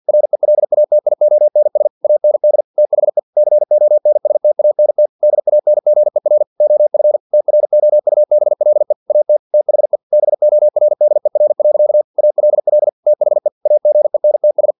Morse code, heard as a keyed tone, about 50 wpm, 600 Hz. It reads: RELATIONS AND THE CONSTANT DANGER OF TROUBLE AT THE BORDER. ALL THE AGENTS
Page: Never 49wpm